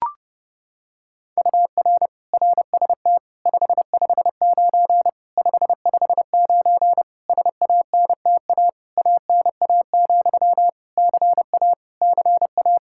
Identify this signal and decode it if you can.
Morse code, heard as a keyed tone, about 30 wpm, 700 Hz. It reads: UR RST 559 559 SANTA ANA, CA CA
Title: Practice QSO - Exchange 1 - Beginning - 30wpm